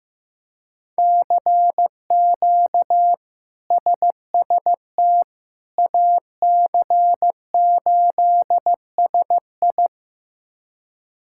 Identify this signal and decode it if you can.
Morse code, heard as a keyed tone, about 15 wpm, 700 Hz. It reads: CQ SST AC8SI
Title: K1USN SST QSO Practice - 15wpm